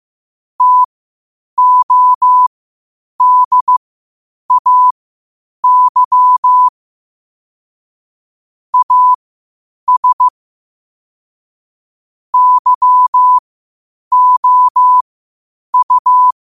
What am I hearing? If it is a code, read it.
Morse code, heard as a keyed tone, about 15 wpm, spaced as for 8 wpm, 1000 Hz. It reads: TODAY AS YOU
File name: qotdfiles / 8wpm